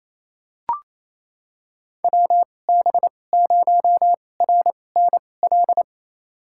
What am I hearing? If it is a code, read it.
Morse code, heard as a keyed tone, about 28 wpm, spaced as for 20 wpm, 700 Hz. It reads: WB0RNL
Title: USA Call Signs - 6 Characters - 2x Character Spacing - 28wpm